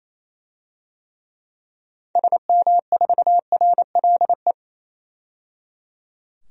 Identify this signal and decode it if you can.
Morse code, heard as a keyed tone, about 28 wpm, 700 Hz. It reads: SM4RLE
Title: DXCC-100 Call Signs - 6 Characters - 28wpm